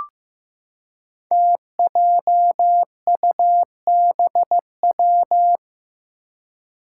Morse code, heard as a keyed tone, about 15 wpm, 700 Hz. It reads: TJUBW
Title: Five Challenging Letter-Number Combinations 15wpm